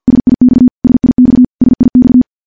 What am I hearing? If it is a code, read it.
Morse code, heard as a keyed tone, about 50 wpm, 250 Hz. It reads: SIX SIX SIX
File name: morse / morse-training-50WPM